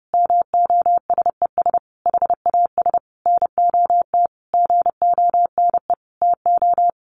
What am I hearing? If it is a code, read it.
Morse code, heard as a keyed tone, about 30 wpm, 700 Hz. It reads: MOSES HAS NOT GONE TO
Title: A Sound Of Thunder 30wpm